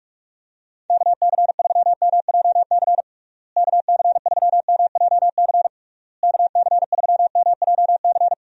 Morse code, heard as a keyed tone, about 46 wpm, 700 Hz. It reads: KC3MJC KC3MJC KC3MJC
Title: USA Call Signs - 6 Characters - Repeated 3x - 45wpm